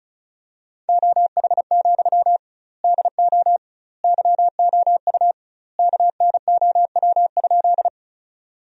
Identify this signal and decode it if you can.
Morse code, heard as a keyed tone, about 34 wpm, 700 Hz. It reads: OH, DO YOU KNOW?
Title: 2x Word-Spacing - Sentences from Top 500 Words - 35wpm